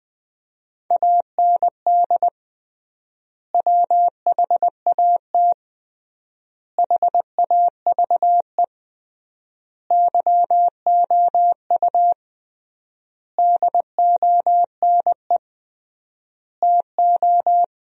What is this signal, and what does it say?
Morse code, heard as a keyed tone, about 20 wpm, 700 Hz. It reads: AND WHAT HAVE YOU DONE TO
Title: 3x Word-Spacing - Sentences from Top 500 Words - 20wpm